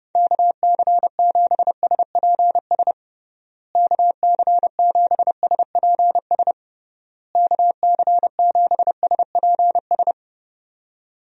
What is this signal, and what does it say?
Morse code, heard as a keyed tone, about 31 wpm, 700 Hz. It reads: KC7SPS KC7SPS KC7SPS
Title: USA Call Signs - 6 Characters - Repeated 3x - 30wpm